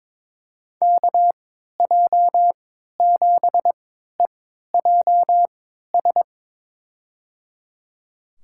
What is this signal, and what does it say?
Morse code, heard as a keyed tone, about 22 wpm, spaced as for 13 wpm, 700 Hz. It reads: KJ7EJS
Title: USA Call Signs - 6 Characters - 3x Character Spacing - 22wpm